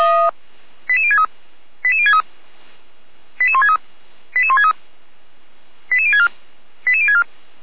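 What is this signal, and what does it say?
Five-tone selective call (ZVEI), heard as quick runs of five tones, tones of about 70 ms. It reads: 88063 88063 88163 88163 88064 88064
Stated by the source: ZVEI five-tone selective call received by SDR, Feuerwehr Orlamünde fire brigade.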